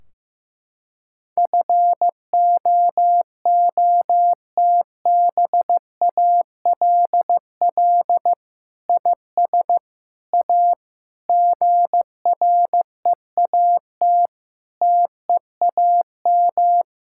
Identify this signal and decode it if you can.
Morse code, heard as a keyed tone, about 15 wpm, 700 Hz. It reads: FOOTBALL IS A GREAT TEAM
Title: Sentences from Top 1600 Words 15wpm